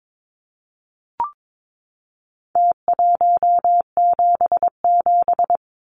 Morse code, heard as a keyed tone, about 22 wpm, 700 Hz. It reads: T177
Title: Numbers - Contest Style - 4-Digit 22wpm